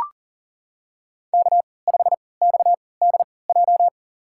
Morse code, heard as a keyed tone, about 40 wpm, spaced as for 23 wpm, 700 Hz. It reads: K5XDJ
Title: USA Call Signs - 5 Characters - 3x Character Spacing - 40wpm